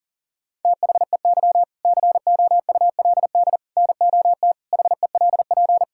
Morse code, heard as a keyed tone, about 40 wpm, 700 Hz. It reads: THEY COULD NOT HELP
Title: Sentences from Top 1000 Words - Review Part 5 - 40wpm